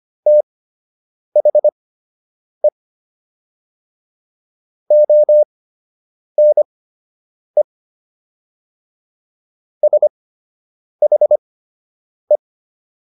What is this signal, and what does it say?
Morse code, heard as a keyed tone, about 25 wpm, spaced as for 8 wpm, 600 Hz. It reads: THE ONE SHE